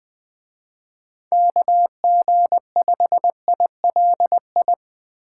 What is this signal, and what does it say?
Morse code, heard as a keyed tone, about 20 wpm, 700 Hz. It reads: KG5ILI
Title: Speed Racing - USA 2x3 Call Signs 20wpm